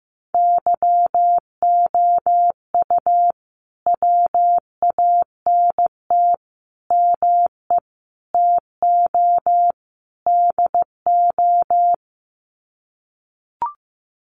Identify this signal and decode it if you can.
Morse code, heard as a keyed tone, about 15 wpm, 700 Hz. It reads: YOU WANT ME TO DO
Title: Sets of 5 Words Encore2 15wpm